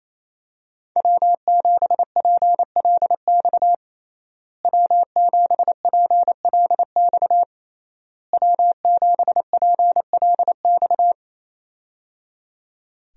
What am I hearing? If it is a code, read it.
Morse code, heard as a keyed tone, about 27 wpm, 700 Hz. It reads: W7PLX W7PLX W7PLX
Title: USA Call Signs - 5 Characters - Repeated 3x - 28wpm